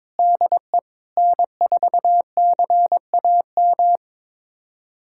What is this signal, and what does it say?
Morse code, heard as a keyed tone, about 21 wpm, 700 Hz. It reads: DE N4CAM
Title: USA Callsign DE Callsign - 5 Characters - 22wpm